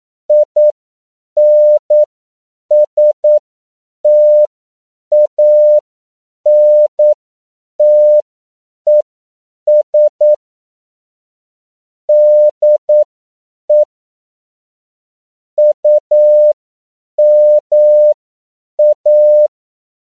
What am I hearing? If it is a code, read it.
Morse code, heard as a keyed tone, about 9 wpm, spaced as for 7 wpm, 600 Hz. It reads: INSTANTES DE UMA
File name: CW / 10PPM